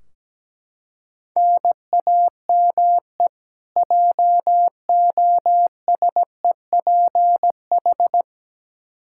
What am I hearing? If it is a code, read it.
Morse code, heard as a keyed tone, about 17 wpm, 700 Hz. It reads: NAME JOSEPH
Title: Practice QSO - Exchange 1 - Beginning - 17wpm